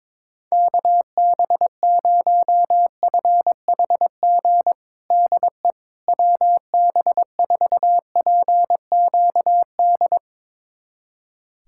Morse code, heard as a keyed tone, about 22 wpm, 700 Hz. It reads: KB0FHG DE WB4PQD
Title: USA Callsign DE Callsign - 6 Characters - 22wpm